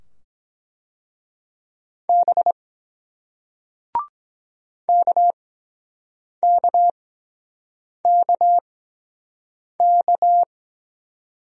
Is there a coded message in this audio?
Yes: Morse code from 2.1 to 10.4 s, heard as a keyed tone, about 23 wpm, spaced as for 7 wpm, 700 Hz. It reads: B KKKK